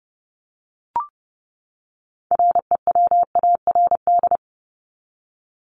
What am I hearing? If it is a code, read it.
Morse code, heard as a keyed tone, about 30 wpm, 700 Hz. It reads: REWARD
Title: Top 1751-2000 Words 30wpm